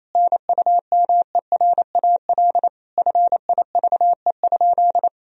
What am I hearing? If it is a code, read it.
Morse code, heard as a keyed tone, about 28 wpm, 700 Hz. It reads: NUMERAL FIVE?
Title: Sentences from Top 1000 Words - Review Part 2 - 28wpm